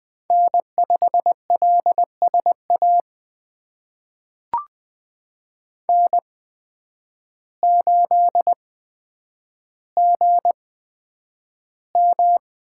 Morse code, heard as a keyed tone, about 20 wpm, spaced as for 9 wpm, 700 Hz. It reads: N5LSA N8GM
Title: USA Call Signs - 5 Characters - 8x Character Spacing - 20wpm